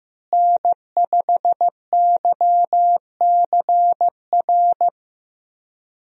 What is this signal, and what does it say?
Morse code, heard as a keyed tone, about 15 wpm, 700 Hz. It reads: N5YCR
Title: USA Call Signs - 5 Characters - Repeated 2x - 15wpm